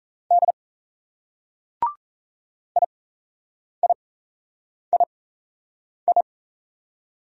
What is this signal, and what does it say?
Morse code, heard as a keyed tone, about 36 wpm, spaced as for 8 wpm, 700 Hz. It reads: D IIII